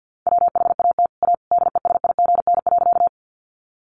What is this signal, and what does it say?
Morse code, heard as a keyed tone, about 50 wpm, 700 Hz. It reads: WHAT A DESIGN.
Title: Sentences from Top 800 Words 50wpm